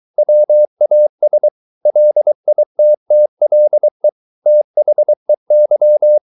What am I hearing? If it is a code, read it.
Morse code, heard as a keyed tone, about 23 wpm, 600 Hz. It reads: WAS LITTLE THEY